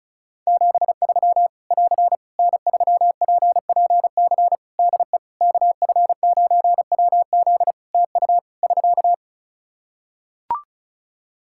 Morse code, heard as a keyed tone, about 36 wpm, 700 Hz. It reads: 73 <AR> N3PPC DE KF9WZ TU <SK>
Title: K1USN SST QSO Practice 35wpm